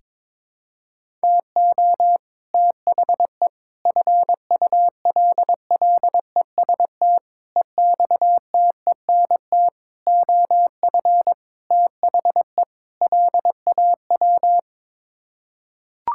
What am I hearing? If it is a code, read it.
Morse code, heard as a keyed tone, about 22 wpm, 700 Hz. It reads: TO THE FULLEST EXTENT OF THE LAW
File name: practice / 22wpm / Sets of 7 Words - 22wpm